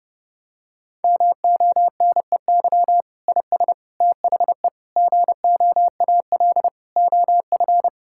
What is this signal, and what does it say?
Morse code, heard as a keyed tone, about 30 wpm, 700 Hz. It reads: MONEY IS THE GOAL OF
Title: Sentences from Top 1000 Words - Review Part 5 - 30wpm